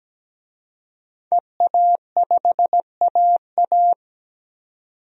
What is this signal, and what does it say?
Morse code, heard as a keyed tone, about 17 wpm, 700 Hz. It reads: EA5AA
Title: DXCC-100 Call Signs - 5 Characters - 17wpm